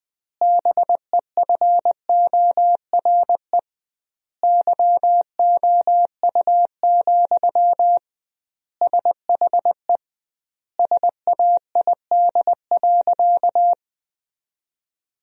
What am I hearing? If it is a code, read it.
Morse code, heard as a keyed tone, about 20 wpm, 700 Hz. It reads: BEFORE YOU, SHE SAID.
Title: 2x Word-Spacing - Sentences from Top 400 Words - 20wpm